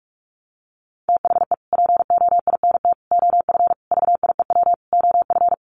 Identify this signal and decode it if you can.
Morse code, heard as a keyed tone, about 44 wpm, 700 Hz. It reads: THE POINT OF VIEW OF
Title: Sets of 5 Words 45wpm